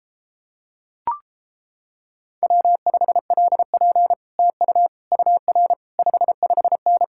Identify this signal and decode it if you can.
Morse code, heard as a keyed tone, about 31 wpm, 700 Hz. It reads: W5LP TU UR 55N